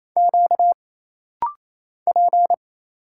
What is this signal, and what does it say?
Morse code, heard as a keyed tone, about 27 wpm, 700 Hz. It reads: Q P
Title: Single Letters - Rapid-Fire 28wpm